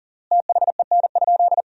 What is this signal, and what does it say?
Morse code, heard as a keyed tone, about 40 wpm, 700 Hz. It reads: THEN?
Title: 3x Word-Spacing - Sentences from Top 100 Words - 40wpm